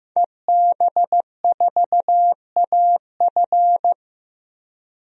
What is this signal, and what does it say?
Morse code, heard as a keyed tone, about 15 wpm, 700 Hz. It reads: EB4AF
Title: DXCC-100 Call Signs - 5 Characters - 15wpm